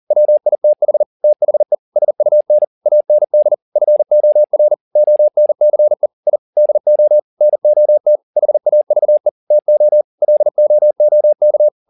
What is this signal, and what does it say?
Morse code, heard as a keyed tone, about 40 wpm, 600 Hz. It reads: WITH THE SUN AND FOR ONCE I DO NOT HAVE TO LOOK